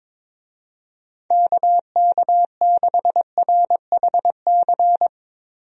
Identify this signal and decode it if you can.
Morse code, heard as a keyed tone, about 22 wpm, 700 Hz. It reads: KK6RHC
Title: USA Call Signs - 6 Characters - Repeated 3x - 22wpm